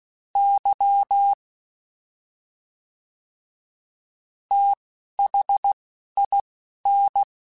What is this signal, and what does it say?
Morse code, heard as a keyed tone, about 16 wpm, spaced as for 6 wpm, 800 Hz. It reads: YTHIN
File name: Special INGs / Special INGs-mp3s / Fast ing @10